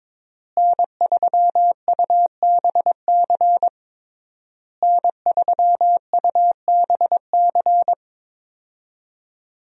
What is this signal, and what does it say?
Morse code, heard as a keyed tone, about 22 wpm, 700 Hz. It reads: N3UBC N3UBC
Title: USA Call Signs - 5 Characters - Repeated 3x - 22wpm